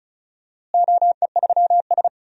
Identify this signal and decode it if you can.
Morse code, heard as a keyed tone, about 36 wpm, 700 Hz. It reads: OE3S
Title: DXCC-100 Call Signs - 4 Characters - Repeated 3x - 35wpm